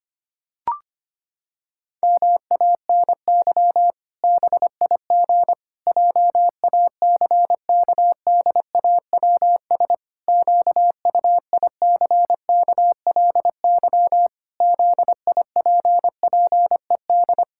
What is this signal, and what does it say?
Morse code, heard as a keyed tone, about 25 wpm, 700 Hz. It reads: MANY BIG JACKDAWS QUICKLY ZIPPED
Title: Pangram Challenge v2 - 25wpm